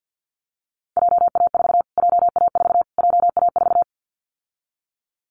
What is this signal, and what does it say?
Morse code, heard as a keyed tone, about 50 wpm, 700 Hz. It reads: JA4 JA4 JA4
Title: Callsign Prefixes for Top 25 DXCC - Repeated 3x - 50wpm